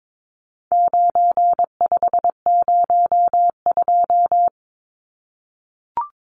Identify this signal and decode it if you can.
Morse code, heard as a keyed tone, about 22 wpm, 700 Hz. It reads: 9502
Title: Numbers - 4-Digit 22wpm